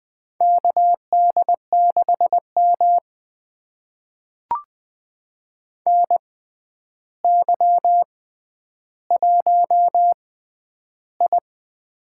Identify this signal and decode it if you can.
Morse code, heard as a keyed tone, about 20 wpm, spaced as for 10 wpm, 700 Hz. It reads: KD6M NY1I
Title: USA Call Signs - 4 Characters - 6x Character Spacing - 20wpm